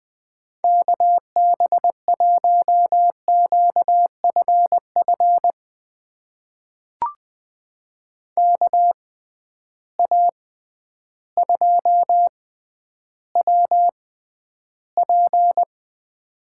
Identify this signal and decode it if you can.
Morse code, heard as a keyed tone, about 20 wpm, spaced as for 11 wpm, 700 Hz. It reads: KB1QFF KA2WP
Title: USA Call Signs - 6 Characters - 6x Character Spacing - 20wpm